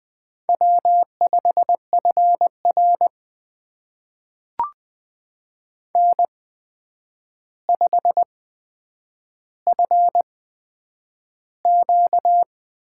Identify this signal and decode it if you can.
Morse code, heard as a keyed tone, about 20 wpm, spaced as for 9 wpm, 700 Hz. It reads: W5FR N5FQ
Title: USA Call Signs - 4 Characters - 8x Character Spacing - 20wpm